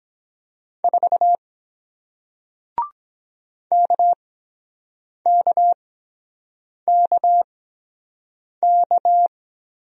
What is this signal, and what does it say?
Morse code, heard as a keyed tone, about 23 wpm, spaced as for 7 wpm, 700 Hz. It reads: 4 KKKK